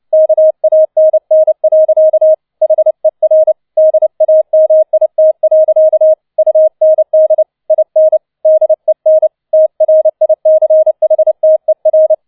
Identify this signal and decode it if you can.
Morse code, heard as a keyed tone, about 29 wpm, 630 Hz. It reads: KANN. HER DAMIT. UND IN DEN TRICHTER